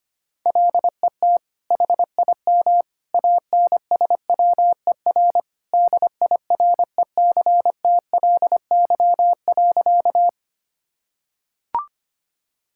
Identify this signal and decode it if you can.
Morse code, heard as a keyed tone, about 25 wpm, 700 Hz. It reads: LET HIM ANSWER DIRECTLY.
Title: Sentences from Top 1000 Words - Review Part 5 - 25wpm